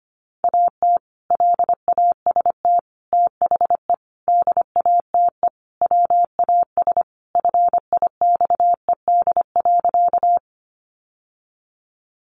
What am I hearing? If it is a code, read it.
Morse code, heard as a keyed tone, about 25 wpm, 700 Hz. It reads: AT LAST THE DATE WAS FIXED.
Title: Sentences from Top 1100 Words 25wpm